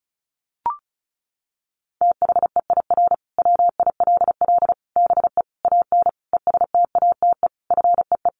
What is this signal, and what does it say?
Morse code, heard as a keyed tone, about 34 wpm, 700 Hz. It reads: THEIR WILL BE AN ESTATE FEE
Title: Sentences from Top 1500 Words 35wpm